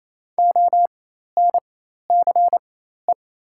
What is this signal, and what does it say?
Morse code, heard as a keyed tone, about 28 wpm, spaced as for 13 wpm, 700 Hz. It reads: ONCE
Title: 4x Character Spacing - Top 500 Words - 28wpm